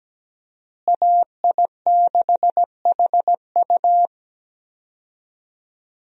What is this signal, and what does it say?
Morse code, heard as a keyed tone, about 17 wpm, 700 Hz. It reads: AI6HU
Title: USA Call Signs - 5 Characters - Repeated 2x - 17wpm